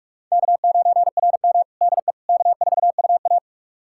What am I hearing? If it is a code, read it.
Morse code, heard as a keyed tone, about 46 wpm, 700 Hz. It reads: K9RM DE K4UA
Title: USA Callsign DE Callsign - 4 Characters - 45wpm